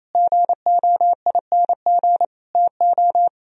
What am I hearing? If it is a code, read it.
Morse code, heard as a keyed tone, about 34 wpm, 700 Hz. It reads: GOING TO
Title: Speed-Racing - Sets of 5 Words 28wpm